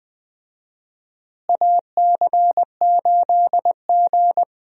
Morse code, heard as a keyed tone, about 21 wpm, 700 Hz. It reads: AC8G
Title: USA Call Signs - 4 Characters - Repeated 2x - 20wpm